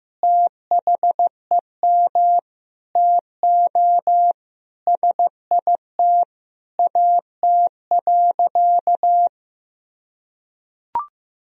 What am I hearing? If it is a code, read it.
Morse code, heard as a keyed tone, about 16 wpm, 700 Hz. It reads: THEM TO SIT AT.
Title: Sentences from Top 1000 Words - Review Part 2 - 15wpm